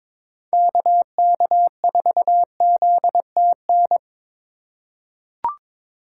Morse code, heard as a keyed tone, about 23 wpm, 700 Hz. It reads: KK4ZTN
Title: USA Call Signs - 6 Characters - 3x Character Spacing - 22wpm